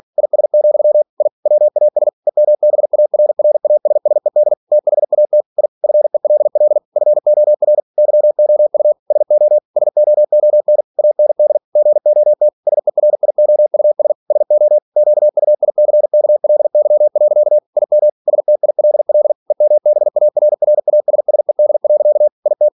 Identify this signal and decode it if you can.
Morse code, heard as a keyed tone, about 46 wpm, 600 Hz. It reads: IS, I WAS EMBARRASSED THAT I FELL FOR YOU SO SOON AND GOT SERIOUS SO QUICKLY. IM STILL EMBARRASSED. IT
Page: Never 47wpm